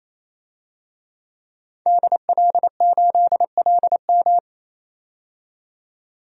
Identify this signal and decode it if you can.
Morse code, heard as a keyed tone, about 28 wpm, 700 Hz. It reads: DL8LM
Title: DXCC-100 Call Signs - 5 Characters - 28wpm